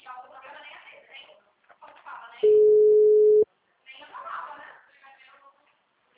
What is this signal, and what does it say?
Ringback tone (Europe, 425 Hz), with speech.
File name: out- -2003- - - .37188